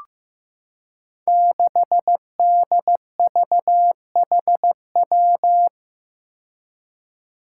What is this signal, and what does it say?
Morse code, heard as a keyed tone, about 16 wpm, 700 Hz. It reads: 6DVHW
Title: Five Challenging Letter-Number Combinations 15wpm